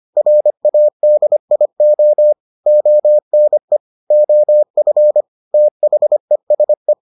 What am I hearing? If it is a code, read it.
Morse code, heard as a keyed tone, about 25 wpm, 600 Hz. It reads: RADIO ONE OF THESE